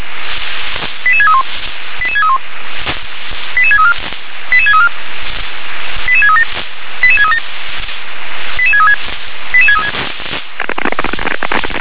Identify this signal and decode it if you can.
Five-tone selective call (ZVEI), heard as quick runs of five tones, tones of about 70 ms. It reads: 88631 88631 88634 88634 88637 88637 88636 88636
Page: /357/SDR/ZVEI/88730/2023.02.01/